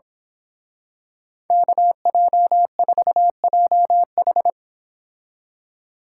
Morse code, heard as a keyed tone, about 26 wpm, 700 Hz. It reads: KJ4JH
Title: Speed Racing - USA 2x2 Call Signs 22wpm